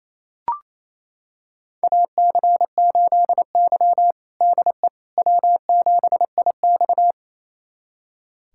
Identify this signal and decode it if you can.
Morse code, heard as a keyed tone, about 28 wpm, 700 Hz. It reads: AC8Y DE W7IX
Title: USA Callsign DE Callsign - 4 Characters - 28wpm